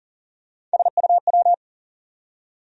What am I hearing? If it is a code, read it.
Morse code, heard as a keyed tone, about 40 wpm, 700 Hz. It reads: SUW
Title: USA Call Sign Postfixes - 3 Letters 40wpm